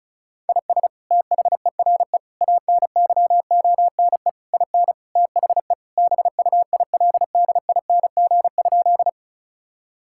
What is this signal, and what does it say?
Morse code, heard as a keyed tone, about 34 wpm, 700 Hz. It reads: IS THERE ANYONE IN THE BUILDING?
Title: Sentences from Top 1000 Words - Review Part 5 - 35wpm